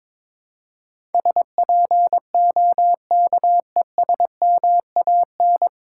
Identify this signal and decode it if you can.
Morse code, heard as a keyed tone, about 22 wpm, 700 Hz. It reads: SPOKESMAN
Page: Morse Code Ninja practice 22wpm